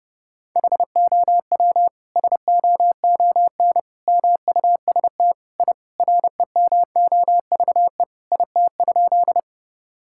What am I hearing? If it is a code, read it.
Morse code, heard as a keyed tone, about 32 wpm, 700 Hz. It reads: HOW SOON MUST I REMOVE IT?
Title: Sentences from Top 1000 Words - Review Part 5 - 30wpm